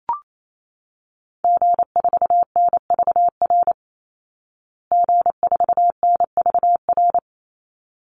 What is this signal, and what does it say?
Morse code, heard as a keyed tone, about 28 wpm, 700 Hz. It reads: G4NVR G4NVR
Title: Call Signs Repeated 3x - 28wpm